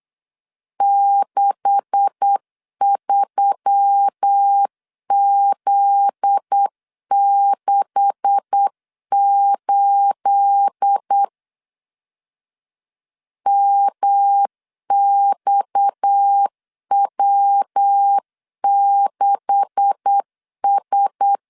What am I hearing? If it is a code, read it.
Morse code, heard as a keyed tone, about 8 wpm, 800 Hz. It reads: 63Z68 MXW6S
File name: LEZIONI / 9 286 8wpm 3425